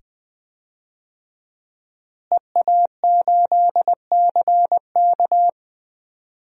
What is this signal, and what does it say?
Morse code, heard as a keyed tone, about 20 wpm, 700 Hz. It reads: EA8CK
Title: DXCC-100 Call Signs - 5 Characters - Repeated 2x - 20wpm